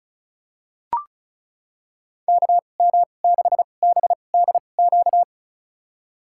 Morse code, heard as a keyed tone, about 36 wpm, spaced as for 25 wpm, 700 Hz. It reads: KM6BDQ
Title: USA Call Signs - 6 Characters - 2x Character Spacing - 35wpm